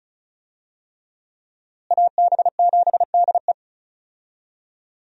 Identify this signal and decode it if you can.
Morse code, heard as a keyed tone, about 34 wpm, 700 Hz. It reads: AB7DE
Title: USA Call Signs - 5 Characters - Repeated 3x - 35wpm